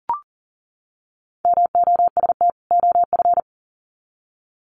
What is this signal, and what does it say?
Morse code, heard as a keyed tone, about 40 wpm, 700 Hz. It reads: MOST OF
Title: Sets of 2 Words 40wpm